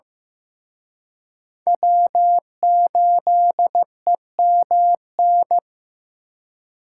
Morse code, heard as a keyed tone, about 18 wpm, 700 Hz. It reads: W8EMN
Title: Speed Racing - USA Call Signs 15wpm